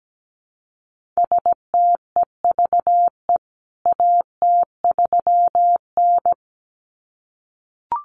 Morse code, heard as a keyed tone, about 17 wpm, 700 Hz. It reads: STEVE AT3N
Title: CWT Exchange 17wpm